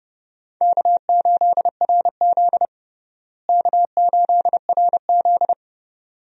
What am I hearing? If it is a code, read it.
Morse code, heard as a keyed tone, about 30 wpm, 700 Hz. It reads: K8RZ K8RZ
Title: USA Call Signs - 4 Characters - Repeated 2x - 30wpm